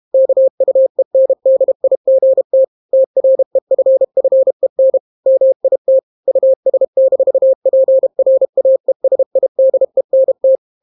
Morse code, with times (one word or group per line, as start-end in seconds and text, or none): 0.1-2.6 KUENDIGT
2.9-5.0 TREFFEN
5.3-6.0 MIT
6.3-10.6 US-PRAESIDENT